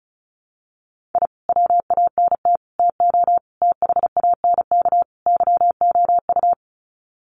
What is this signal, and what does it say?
Morse code, heard as a keyed tone, about 34 wpm, 700 Hz. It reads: I WANT TO THANK YOU
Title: Sets of 5 Words Encore 35wpm